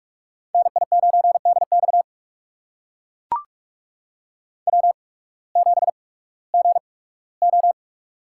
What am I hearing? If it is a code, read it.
Morse code, heard as a keyed tone, about 44 wpm, spaced as for 21 wpm, 700 Hz. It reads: NI9DX W7GO
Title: USA Call Signs - 8x Character Spacing - 45wpm